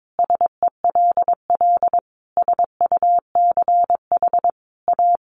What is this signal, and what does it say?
Morse code, heard as a keyed tone, about 22 wpm, 700 Hz. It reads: SELL SUCH A
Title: Sentences from Top 1100 Words 22wpm